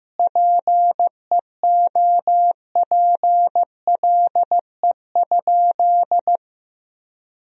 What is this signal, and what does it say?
Morse code, heard as a keyed tone, about 15 wpm, 700 Hz. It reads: PEOPLE?
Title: Sentences from Top 1000 Words - Review Part 1 - 15wpm